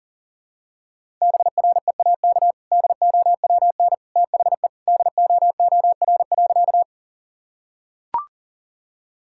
Morse code, heard as a keyed tone, about 40 wpm, 700 Hz. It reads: BREAK DOWN THE DOOR.
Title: Sentences from Top 1000 Words - Review Part 6 - 40wpm